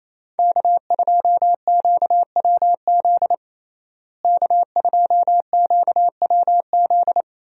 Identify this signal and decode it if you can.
Morse code, heard as a keyed tone, about 28 wpm, 700 Hz. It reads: K2QWZ K2QWZ
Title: USA Call Signs - 5 Characters - Repeated 2x - 28wpm